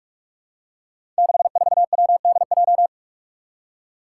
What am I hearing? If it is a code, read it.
Morse code, heard as a keyed tone, about 46 wpm, 700 Hz. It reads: 64WDJ
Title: Five Challenging Letter-Number Combinations 45wpm